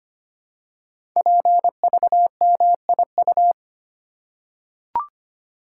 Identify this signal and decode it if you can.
Morse code, heard as a keyed tone, about 25 wpm, 700 Hz. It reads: PVMIU
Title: Code Groups - 5 Letter:Numbers - 25wpm